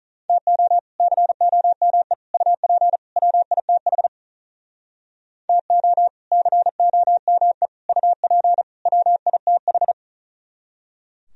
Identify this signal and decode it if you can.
Morse code, heard as a keyed tone, about 47 wpm, 700 Hz. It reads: TO COME UP WITH TO COME UP WITH